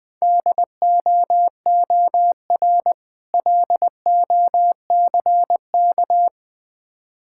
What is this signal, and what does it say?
Morse code, heard as a keyed tone, about 20 wpm, 700 Hz. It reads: DOOR LOCK
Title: Easy Words - Speed-Racing with Hints - 20wpm